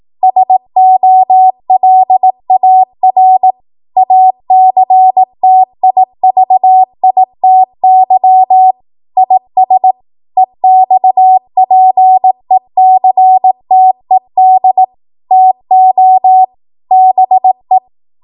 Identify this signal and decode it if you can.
Morse code, heard as a keyed tone, about 18 wpm, 750 Hz. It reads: SOLAR ACTIVITY IS EXPECTED TO BE
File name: Morse / 18WPM